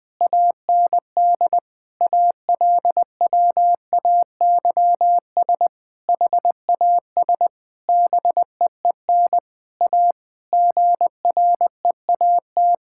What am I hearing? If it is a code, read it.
Morse code, heard as a keyed tone, about 20 wpm, 700 Hz. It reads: AND ALWAYS HAS BEEN A GREAT
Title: Sentences from Top 1000 Words - Review Part 6 - 20wpm